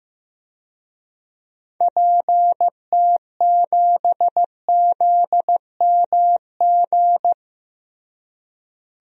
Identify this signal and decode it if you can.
Morse code, heard as a keyed tone, about 15 wpm, 700 Hz. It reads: PT7ZMG
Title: DXCC-100 Call Signs - 6 Characters - Repeated 2x - 15wpm